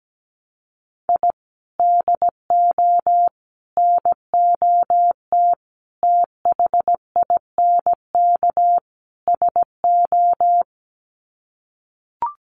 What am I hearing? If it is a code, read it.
Morse code, heard as a keyed tone, about 17 wpm, 700 Hz. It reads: I DO NOT THINK SO
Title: Sets of 5 Words 17wpm